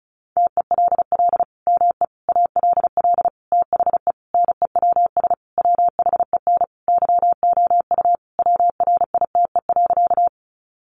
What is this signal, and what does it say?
Morse code, heard as a keyed tone, about 34 wpm, 700 Hz. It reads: TELL ME ALL THE NEWS WHEN YOU WRITE.
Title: Sentences from Top 800 Words 35wpm